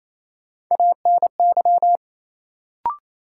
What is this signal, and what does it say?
Morse code, heard as a keyed tone, about 28 wpm, 700 Hz. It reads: ANY
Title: Sets of 3 Words - New Word Drill - 28wpm